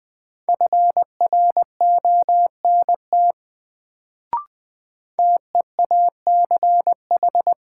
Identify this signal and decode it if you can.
Morse code, heard as a keyed tone, about 20 wpm, 700 Hz. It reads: FRONT TEACH
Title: Sentences from Top 500 Words - New Word Drill - 20wpm